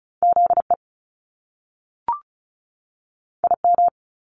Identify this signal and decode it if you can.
Morse code, heard as a keyed tone, about 36 wpm, 700 Hz. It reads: ZE IM
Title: Top Two Letter Combinations in Words 35wpm